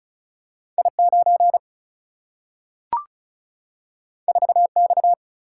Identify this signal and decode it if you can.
Morse code, heard as a keyed tone, about 36 wpm, 700 Hz. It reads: I 9 4 X
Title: Single-Letter Single-Number 35wpm